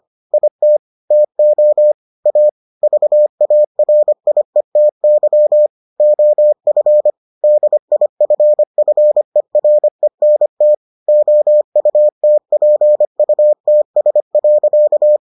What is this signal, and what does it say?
Morse code, heard as a keyed tone, about 25 wpm, 600 Hz. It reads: IT TO A VARIETY OF DIFFERENT OUTPUTS.